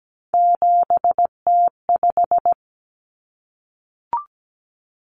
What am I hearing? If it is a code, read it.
Morse code, heard as a keyed tone, about 17 wpm, 700 Hz. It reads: 7T5
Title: Common Cut Numbers - 3-Digit 17wpm